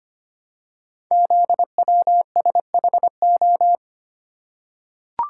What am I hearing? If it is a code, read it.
Morse code, heard as a keyed tone, about 25 wpm, 700 Hz. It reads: ZWSHO
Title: Code Groups - 5 Letters - 25wpm